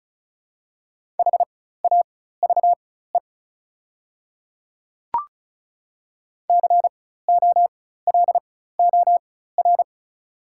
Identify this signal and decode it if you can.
Morse code, heard as a keyed tone, about 34 wpm, spaced as for 16 wpm, 700 Hz. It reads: HAVE COLOR
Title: 4x Character Spacing - Top 300 Words - 35wpm